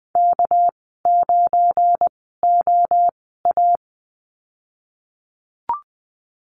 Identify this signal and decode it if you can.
Morse code, heard as a keyed tone, about 20 wpm, spaced as for 15 wpm, 700 Hz. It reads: K9OA
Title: 200 Lesson 32g - Letter k - Call Signs 20wpm